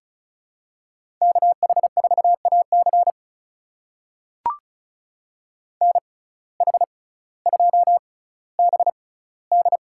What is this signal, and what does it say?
Morse code, heard as a keyed tone, about 36 wpm, spaced as for 18 wpm, 700 Hz. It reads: KH4AC NH2BD
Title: USA Call Signs - 5 Characters - 6x Character Spacing - 35wpm